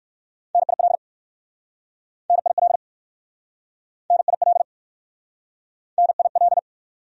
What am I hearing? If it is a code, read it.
Morse code, heard as a keyed tone, about 55 wpm, 700 Hz. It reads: NIL NIL NIL NIL
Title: Speed-Racing Abbreviations - 45wpm